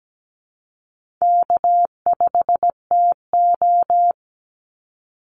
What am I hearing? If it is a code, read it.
Morse code, heard as a keyed tone, about 17 wpm, 700 Hz. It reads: K5TO
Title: Call Signs Repeated 3x - 17wpm